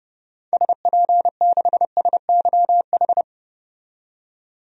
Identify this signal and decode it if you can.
Morse code, heard as a keyed tone, about 32 wpm, 700 Hz. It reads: SP6SYH
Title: DXCC-100 Call Signs - 6 Characters - 30wpm